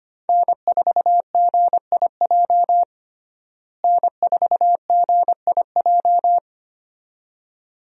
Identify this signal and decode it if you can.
Morse code, heard as a keyed tone, about 25 wpm, 700 Hz. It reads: N4GIJ N4GIJ
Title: USA Call Signs - 5 Characters - Repeated 2x - 25wpm